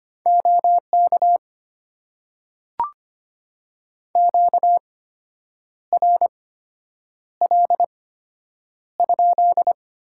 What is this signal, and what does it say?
Morse code, heard as a keyed tone, about 25 wpm, spaced as for 8 wpm, 690 Hz. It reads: OK QRL?
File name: Top 90 QSO Elements - 8x Character Spacing - 25wpm